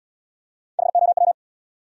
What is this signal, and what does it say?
Morse code, heard as a keyed tone, about 68 wpm, 700 Hz. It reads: HOW